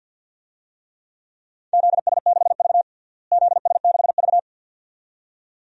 Morse code, heard as a keyed tone, about 54 wpm, 700 Hz. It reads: ZS6V ZS6V
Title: DXCC-100 Call Signs - 4 Characters - Repeated 2x - 50wpm